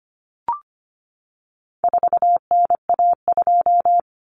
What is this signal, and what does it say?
Morse code, heard as a keyed tone, about 25 wpm, 700 Hz. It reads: 4NA2
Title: Common Cut Numbers - 4-Digit 25wpm